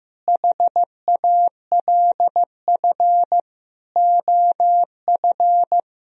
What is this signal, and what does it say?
Morse code, heard as a keyed tone, about 15 wpm, 700 Hz. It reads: HALF OF
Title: Sentences from Top 400 Words and Abbreviations 15wpm